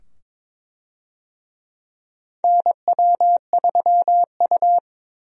Morse code, heard as a keyed tone, about 22 wpm, 700 Hz. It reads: NW3U
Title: USA Call Signs - 4 Characters - 22wpm